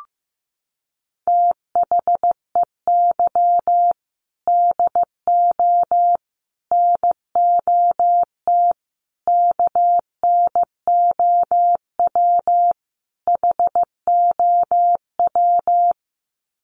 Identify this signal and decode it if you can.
Morse code, heard as a keyed tone, about 15 wpm, 700 Hz. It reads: THEY DO NOT KNOW HOW
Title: Sets of 5 Words Encore2 15wpm